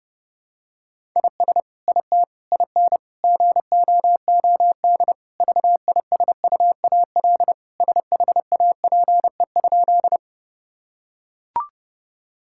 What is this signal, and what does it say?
Morse code, heard as a keyed tone, about 30 wpm, 700 Hz. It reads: IS IT IN GOOD VISUAL SHAPE?
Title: Sentences from Top 1000 Words - Review Part 6 - 30wpm